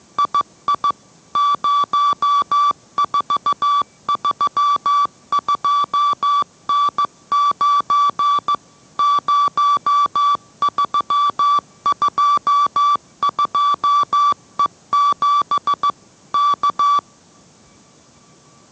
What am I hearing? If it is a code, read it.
Morse code, heard as a keyed tone, about 15 wpm, 1160 Hz. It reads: II 0432N9 0322E7 K